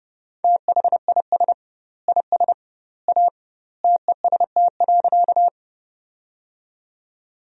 Morse code, heard as a keyed tone, about 30 wpm, 700 Hz. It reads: THIS IS A TEST.
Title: 2x Word-Spacing - Sentences from Top 500 Words - 30wpm